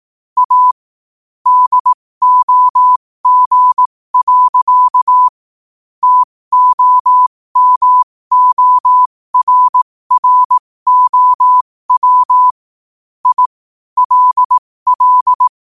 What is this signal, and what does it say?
Morse code, heard as a keyed tone, about 18 wpm, spaced as for 15 wpm, 1000 Hz. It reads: A DOG. TOMORROW I LL
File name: qotdfiles / 15wpm